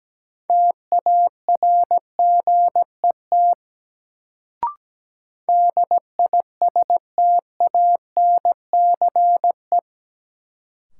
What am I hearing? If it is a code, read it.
Morse code, heard as a keyed tone, about 17 wpm, 700 Hz. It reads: TARGET DISTANCE
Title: Sentences from Top 1200 Words - New Word Drill Alt - 17wpm